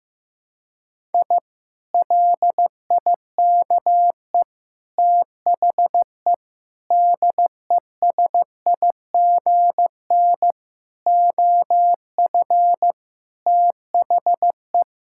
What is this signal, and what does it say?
Morse code, heard as a keyed tone, about 15 wpm, 700 Hz. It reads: I LIKE THE DESIGN OF THE
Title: Sentences from Top 1000 Words - Review Part 5 - 15wpm